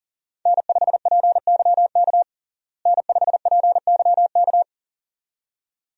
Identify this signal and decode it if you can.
Morse code, heard as a keyed tone, about 40 wpm, 700 Hz. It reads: N5PYK N5PYK
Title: USA Call Signs - 5 Characters - Repeated 2x - 40wpm